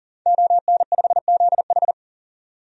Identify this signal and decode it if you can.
Morse code, heard as a keyed tone, about 38 wpm, 700 Hz. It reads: ON5ZH
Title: DXCC-100 Call Signs - 5 Characters - 40wpm